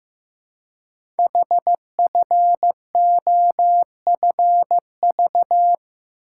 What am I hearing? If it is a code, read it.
Morse code, heard as a keyed tone, about 15 wpm, 700 Hz. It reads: HFOFV
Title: Code Groups - 5 Letters - 15wpm